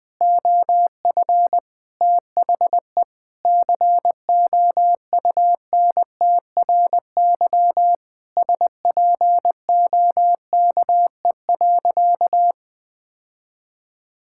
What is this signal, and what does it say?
Morse code, heard as a keyed tone, about 20 wpm, 700 Hz. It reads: OF THE COUNTRY SPOKE.
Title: Sentences with 3-Word Phrases - 20wpm